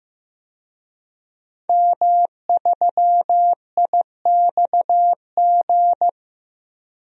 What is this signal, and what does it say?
Morse code, heard as a keyed tone, about 15 wpm, 700 Hz. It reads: M3IXG
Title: DXCC-100 Call Signs - 5 Characters - Repeated 3x - 15wpm